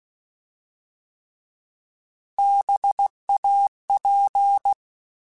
Morse code, heard as a keyed tone, about 16 wpm, 800 Hz. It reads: BAP
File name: Alphabetical / Alphabetical mp3s / Bs @15